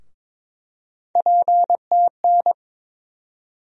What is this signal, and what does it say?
Morse code, heard as a keyed tone, about 22 wpm, 700 Hz. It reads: PTN
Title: USA Call Sign Postfixes - 3 Letters 22wpm